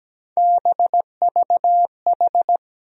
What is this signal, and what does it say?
Morse code, heard as a keyed tone, about 17 wpm, 700 Hz. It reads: BVH
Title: USA Call Sign Postfixes - 3 Letters 17wpm